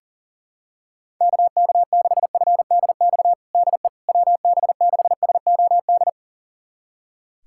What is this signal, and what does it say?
Morse code, heard as a keyed tone, about 40 wpm, 700 Hz. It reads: KK6FDX DE WB6SOD
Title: USA Callsign DE Callsign - 6 Characters - 40wpm